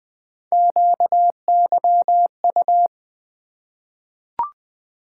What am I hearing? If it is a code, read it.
Morse code, heard as a keyed tone, about 20 wpm, 700 Hz. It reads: QYU
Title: Three Letter-Number Combinations - 20wpm